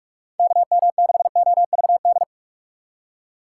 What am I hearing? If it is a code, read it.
Morse code, heard as a keyed tone, about 46 wpm, 700 Hz. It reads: KM6OVD
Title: USA Call Signs - 6 Characters - 4x Character Spacing - 45wpm